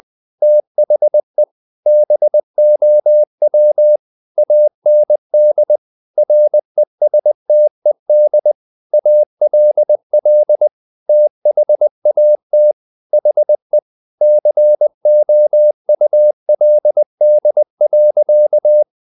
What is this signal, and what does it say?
Morse code, heard as a keyed tone, about 20 wpm, 600 Hz. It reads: THE BOW AND RESTED ALL THAT HE COULD.
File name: the old man and the sea part 2 20wpm